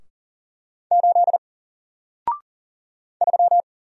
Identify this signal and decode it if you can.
Morse code, heard as a keyed tone, about 40 wpm, 700 Hz. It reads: 8 3
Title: Single Character (Letter:Number:Punctuation:Prosign) 40wpm